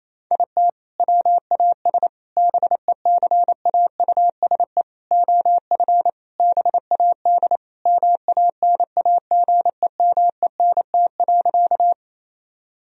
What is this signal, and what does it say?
Morse code, heard as a keyed tone, about 27 wpm, 700 Hz. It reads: IT WAS BECAUSE OF BAD MANAGEMENT.
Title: Sentences from Top 1000 Words - Review Part 4 - 28wpm